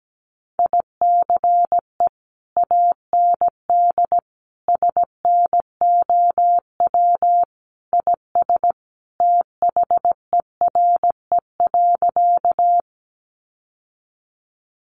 Morse code, heard as a keyed tone, about 17 wpm, 700 Hz. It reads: ICE AND SNOW IS THERE.
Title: Sentences from Top 1000 Words 17wpm